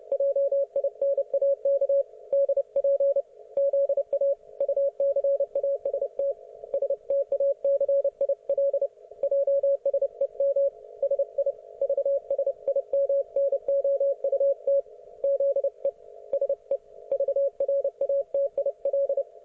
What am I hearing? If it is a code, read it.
Morse code, heard as a keyed tone, about 29 wpm, 560 Hz. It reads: JINAK DP ZA UCAST STACIL JSEM SI VSIMNOUT ZE SE VRATIL
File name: Archiv / CUC133 / oK1fhi cul133